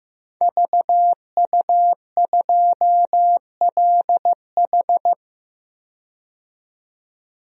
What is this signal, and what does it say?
Morse code, heard as a keyed tone, about 15 wpm, 700 Hz. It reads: VU2LH
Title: DXCC-100 Call Signs - 5 Characters - 15wpm